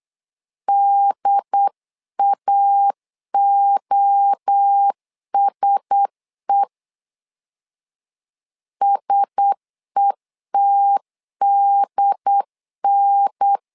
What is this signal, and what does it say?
Morse code, heard as a keyed tone, about 8 wpm, 800 Hz. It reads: DAOSE SETDN
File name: LEZIONI / 4 Q1S 8wpm 2710